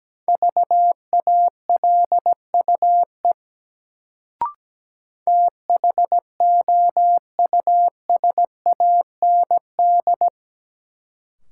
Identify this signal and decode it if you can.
Morse code, heard as a keyed tone, about 17 wpm, 700 Hz. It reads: VALUE THOUSAND
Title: Sentences from Top 600 Words - New Word Drill - 17wpm